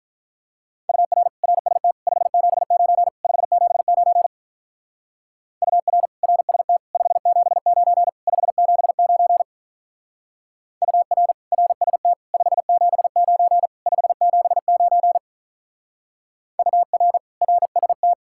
Morse code, heard as a keyed tone, about 43 wpm, 700 Hz. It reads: UR RST 579 579 UR RST 579 579 UR RST 579 579 UR RST
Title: Speed-Racing Common QSO Phrases - 35wpm